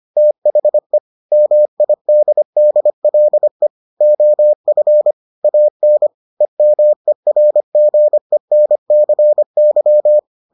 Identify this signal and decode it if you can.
Morse code, heard as a keyed tone, about 25 wpm, 600 Hz. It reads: THE MIDDLE OF AN EMERGENCY